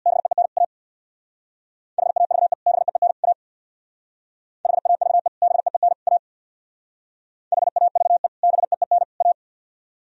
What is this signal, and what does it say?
Morse code, heard as a keyed tone, about 61 wpm, 700 Hz. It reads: BEEN A HAVE BEEN A HAVE BEEN A HAVE BEEN A
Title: Speed-Racing - Sets of 3 Words 50wpm